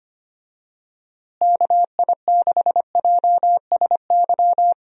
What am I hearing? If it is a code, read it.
Morse code, heard as a keyed tone, about 25 wpm, 700 Hz. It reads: KI6JSY
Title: USA Call Signs - 6 Characters - Repeated 2x - 25wpm